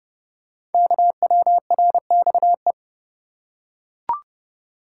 Morse code, heard as a keyed tone, about 30 wpm, 700 Hz. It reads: KWRXE
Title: Code Groups - 5 Letters - 30wpm